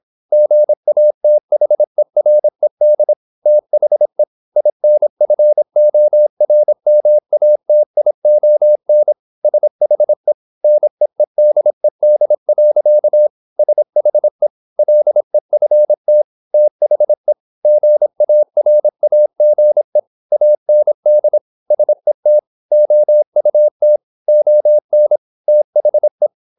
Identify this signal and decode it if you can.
Morse code, heard as a keyed tone, about 26 wpm, 600 Hz. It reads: GATHERED THE INFORMATION SHE NEEDED. SHE LEFT THE GARAGE AND SET OUT ON THE
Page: Never 26wpm | CW med Gnister